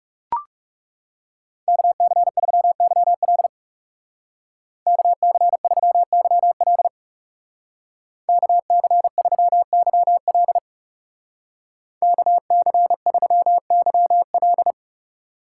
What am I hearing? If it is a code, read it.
Morse code, heard as a keyed tone, about 38 wpm, 700 Hz. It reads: KC3YL KC3YL KC3YL KC3YL